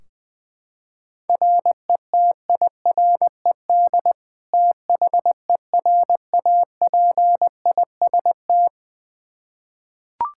Morse code, heard as a keyed tone, about 21 wpm, 700 Hz. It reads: RETIRED THERAPIST
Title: Practice QSO - Exchange 3 - More Info - 20wpm